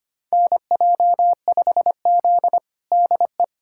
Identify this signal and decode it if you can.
Morse code, heard as a keyed tone, about 25 wpm, 700 Hz. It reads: NJ5Z DE
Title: USA Callsign DE Callsign - 4 Characters - 25wpm